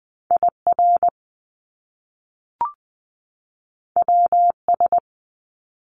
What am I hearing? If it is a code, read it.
Morse code, heard as a keyed tone, about 20 wpm, 700 Hz. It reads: I R W S
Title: Top Two Letter Combinations in Words 20wpm